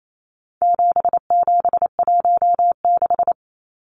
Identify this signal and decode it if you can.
Morse code, heard as a keyed tone, about 28 wpm, 700 Hz. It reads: 7716
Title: Numbers - 4-Digit 28wpm